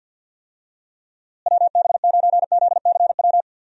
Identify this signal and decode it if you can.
Morse code, heard as a keyed tone, about 52 wpm, 700 Hz. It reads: WB9ZCW
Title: USA Call Signs - 6 Characters - Repeated 2x - 50wpm